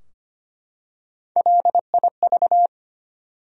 Morse code, heard as a keyed tone, about 25 wpm, 700 Hz. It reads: LIV
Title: USA Common Names - Max 4 Letters - 25wpm